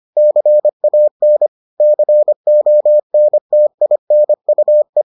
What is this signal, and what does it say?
Morse code, heard as a keyed tone, about 25 wpm, 600 Hz. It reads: CAN CONTINUE